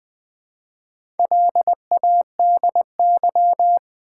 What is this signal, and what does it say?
Morse code, heard as a keyed tone, about 20 wpm, 700 Hz. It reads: LADY
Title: Build A Word - 1 Letter At A Time - 4-Letter Words - 20wpm